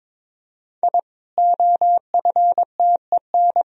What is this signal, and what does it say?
Morse code, heard as a keyed tone, about 21 wpm, 700 Hz. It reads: I OFTEN
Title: Sentences from Top 1000 Words - Review Part 4 - 22wpm